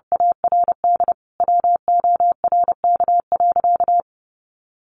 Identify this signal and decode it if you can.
Morse code, heard as a keyed tone, about 30 wpm, 700 Hz. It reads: ARD WORK.
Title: Sentences from Top 600 Words 30wpm